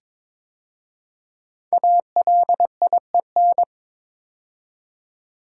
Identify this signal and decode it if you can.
Morse code, heard as a keyed tone, about 22 wpm, 700 Hz. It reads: ALIEN
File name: 5 Letter Words - First 3 Letters Given - 22wpm